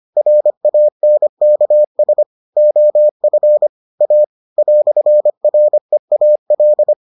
Morse code, heard as a keyed tone, about 25 wpm, 600 Hz. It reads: RANKS OF A "REAL